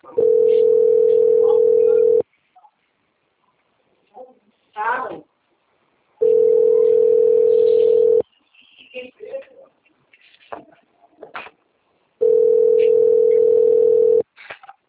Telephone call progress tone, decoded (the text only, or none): ringback tone